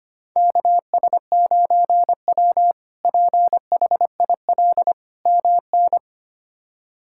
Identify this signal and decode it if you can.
Morse code, heard as a keyed tone, about 25 wpm, 700 Hz. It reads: KS9W PHIL MN
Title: K1USN SST QSO Practice - 25wpm